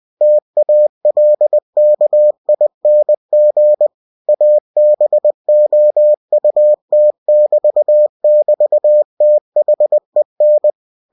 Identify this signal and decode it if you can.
Morse code, heard as a keyed tone, about 20 wpm, 600 Hz. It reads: TALKING ABOUT<BT><BT>THEN
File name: cw-books / winnie-the-pooh / winnie-20wpm-page-0079